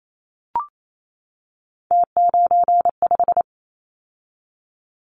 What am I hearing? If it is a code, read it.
Morse code, heard as a keyed tone, about 28 wpm, 700 Hz. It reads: T95
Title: Callsign Prefixes for Top 100 DXCC - 28wpm